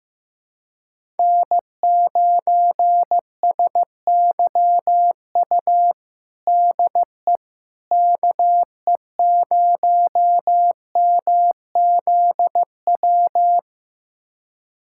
Morse code, heard as a keyed tone, about 16 wpm, 700 Hz. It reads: N9SYU DE KE0MZW
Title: USA Callsign DE Callsign 15wpm